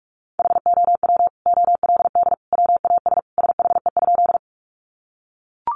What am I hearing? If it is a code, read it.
Morse code, heard as a keyed tone, about 44 wpm, 700 Hz. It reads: HOW OLD WAS SHE?
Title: Sentences from Top 200 Words 45wpm